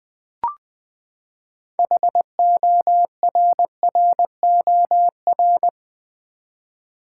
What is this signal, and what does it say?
Morse code, heard as a keyed tone, about 20 wpm, 700 Hz. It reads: HORROR
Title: Six Letter Words - 20wpm